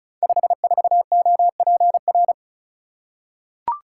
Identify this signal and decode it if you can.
Morse code, heard as a keyed tone, about 36 wpm, 700 Hz. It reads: 54OPR
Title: Code Groups - 5 Letter:Numbers - 35wpm